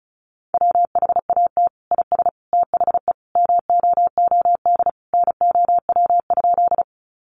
Morse code, heard as a keyed tone, about 34 wpm, 700 Hz. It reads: WHAT IS THE MOOD NOW?
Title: Sentences from Top 1100 Words 35wpm